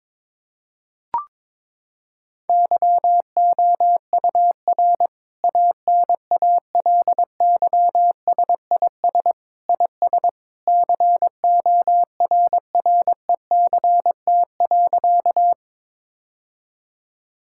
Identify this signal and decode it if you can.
Morse code, heard as a keyed tone, about 22 wpm, 700 Hz. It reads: YOUR ANALYSIS IS CORRECT.
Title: Sentences from Top 1000 Words - Review Part 5 - 22wpm